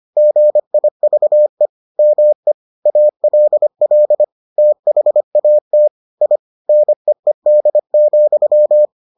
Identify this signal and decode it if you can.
Morse code, heard as a keyed tone, about 25 wpm, 600 Hz. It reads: GIVE ME ALL THAT I NEED,